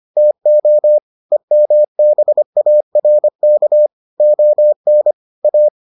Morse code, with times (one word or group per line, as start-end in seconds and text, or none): 0.2-1.0 TO
1.3-3.9 EMBARK
4.2-5.1 ON
5.4-5.7 A